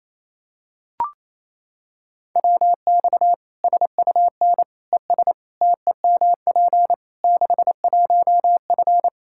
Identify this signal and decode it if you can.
Morse code, heard as a keyed tone, about 28 wpm, 700 Hz. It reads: WX SUN ES TEMP 61F
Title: Practice QSO - Exchange 2 - Info - 28wpm